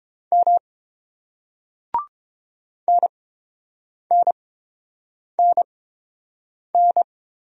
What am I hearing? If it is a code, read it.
Morse code, heard as a keyed tone, about 30 wpm, spaced as for 7 wpm, 700 Hz. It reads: M NNNN